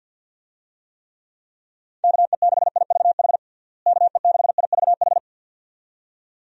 Morse code, heard as a keyed tone, about 50 wpm, 700 Hz. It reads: KE6IVH KE6IVH
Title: USA Call Signs - 6 Characters - Repeated 2x - 50wpm